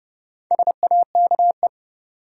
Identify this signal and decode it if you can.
Morse code, heard as a keyed tone, about 30 wpm, 700 Hz. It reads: SAKE
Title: Sets of 4 Words - New Word Drill - 30wpm